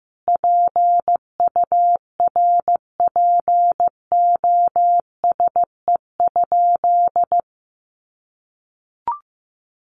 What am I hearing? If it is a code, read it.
Morse code, heard as a keyed tone, about 14 wpm, 700 Hz. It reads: PURPOSE?
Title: Sentences from Top 1200 Words 15wpm